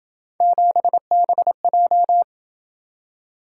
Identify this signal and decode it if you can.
Morse code, heard as a keyed tone, about 27 wpm, 700 Hz. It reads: 7BJ